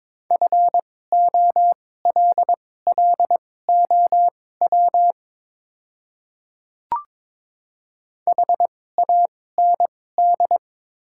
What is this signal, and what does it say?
Morse code, heard as a keyed tone, about 22 wpm, spaced as for 16 wpm, 690 Hz. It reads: FOLLOW HAND
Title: 2x Character Spacing - Top 200 Words - 22wpm